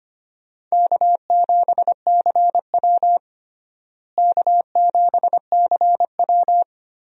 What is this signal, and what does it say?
Morse code, heard as a keyed tone, about 25 wpm, 700 Hz. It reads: K7CW K7CW
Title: USA Call Signs - 4 Characters - Repeated 2x - 25wpm